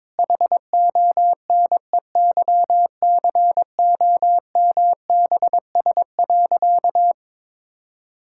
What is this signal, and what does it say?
Morse code, heard as a keyed tone, about 22 wpm, 700 Hz. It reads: HONEYCOMBS.
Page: morse-code-ninja-practice-22wpm